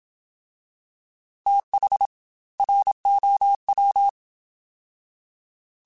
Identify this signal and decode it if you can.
Morse code, heard as a keyed tone, about 27 wpm, 800 Hz. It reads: TH ROW
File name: TH / TH mp3s / TH Prefixes spt @25